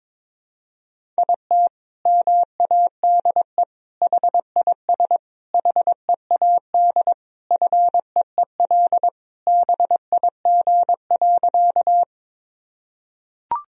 Morse code, heard as a keyed tone, about 22 wpm, 700 Hz. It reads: IT MADE HIS HEAD FEEL BIG.
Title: Sentences from Top 1000 Words - Review Part 1 - 22wpm